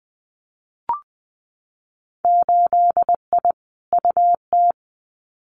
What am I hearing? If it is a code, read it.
Morse code, heard as a keyed tone, about 20 wpm, 700 Hz. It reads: 8I UT
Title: Winter Field Day Exchanges Encore 20wpm